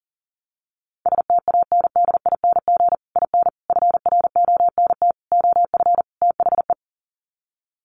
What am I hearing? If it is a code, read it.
Morse code, heard as a keyed tone, about 40 wpm, 700 Hz. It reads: STANDING IN FRONT OF THE
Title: Sets of 5 Words Encore2 40wpm